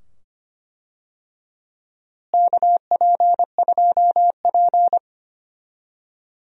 Morse code, heard as a keyed tone, about 25 wpm, 700 Hz. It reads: KP2P
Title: DXCC-100 Call Signs - 4 Characters - Repeated 2x - 25wpm